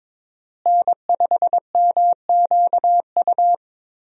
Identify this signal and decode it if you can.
Morse code, heard as a keyed tone, about 22 wpm, 700 Hz. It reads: N5MQU
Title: USA Call Signs - 5 Characters - Repeated 2x - 22wpm